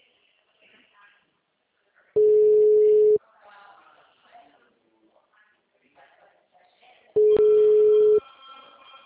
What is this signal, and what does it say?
Ringback tone (Europe, 425 Hz), with speech.
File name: out- -2031- - - .13566